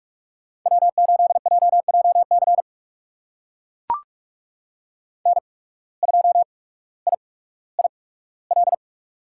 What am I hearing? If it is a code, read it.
Morse code, heard as a keyed tone, about 44 wpm, spaced as for 19 wpm, 700 Hz. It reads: W8JJC N2IIL
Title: USA Call Signs - 5 Characters - 8x Character Spacing - 45wpm